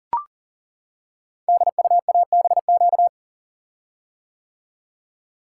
Morse code, heard as a keyed tone, about 39 wpm, 690 Hz. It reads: DUABQ
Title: Code Groups - 5 Letters - 40wpm